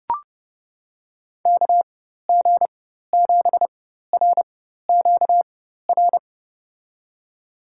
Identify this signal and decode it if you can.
Morse code, heard as a keyed tone, about 29 wpm, spaced as for 14 wpm, 700 Hz. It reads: KG7RQR
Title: USA Call Signs - 4x Character Spacing - 30wpm